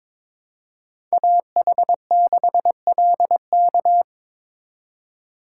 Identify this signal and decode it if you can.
Morse code, heard as a keyed tone, about 27 wpm, 700 Hz. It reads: AH6LK
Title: Speed Racing - USA 2x2 Call Signs 22wpm